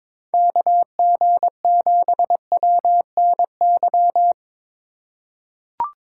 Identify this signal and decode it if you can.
Morse code, heard as a keyed tone, about 22 wpm, 700 Hz. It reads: KG7WNY
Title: USA Call Signs - 6 Characters - 3x Character Spacing - 22wpm